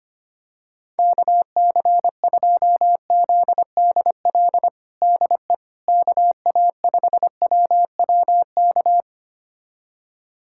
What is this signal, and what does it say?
Morse code, heard as a keyed tone, about 25 wpm, 700 Hz. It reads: KC2ZDL DE KA5WWK
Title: USA Callsign DE Callsign 25wpm